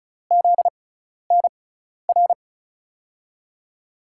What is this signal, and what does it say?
Morse code, heard as a keyed tone, about 36 wpm, spaced as for 12 wpm, 700 Hz. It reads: ZNR
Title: USA Call Signs - 5 Characters - 6x Character Spacing - 35wpm